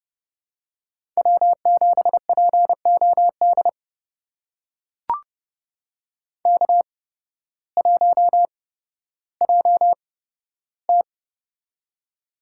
Morse code, heard as a keyed tone, about 30 wpm, spaced as for 14 wpm, 700 Hz. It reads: W7POD K1JT
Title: USA Call Signs - 8x Character Spacing - 30wpm